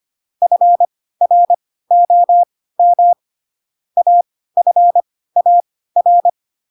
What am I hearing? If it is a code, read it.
Morse code, heard as a keyed tone, about 25 wpm, spaced as for 16 wpm, 700 Hz. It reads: FROM AFAR
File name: Pinocchio / Pinocchio ch3 10 25 ik1hge